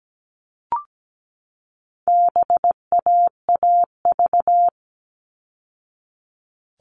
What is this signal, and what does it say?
Morse code, heard as a keyed tone, about 17 wpm, 700 Hz. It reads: BAAV
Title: All Cut Numbers - 4-Digit 17wpm